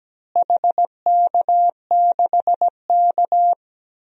Morse code, heard as a keyed tone, about 17 wpm, 700 Hz. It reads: HK6K
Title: DXCC-100 Call Signs - 4 Characters - Repeated 2x - 17wpm